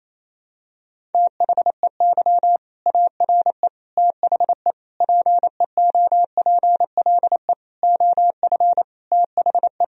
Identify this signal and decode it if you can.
Morse code, heard as a keyed tone, about 28 wpm, 700 Hz. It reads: THEY ARE THE PEOPLE OF THE
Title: Sentences from Top 1000 Words - Review Part 1 - 28wpm